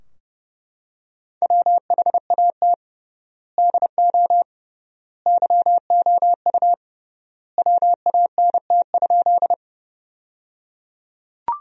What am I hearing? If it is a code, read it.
Morse code, heard as a keyed tone, about 30 wpm, 700 Hz. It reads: WHAT DO YOU WANT?
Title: 3x Word-Spacing - Sentences from Top 200 Words - 30wpm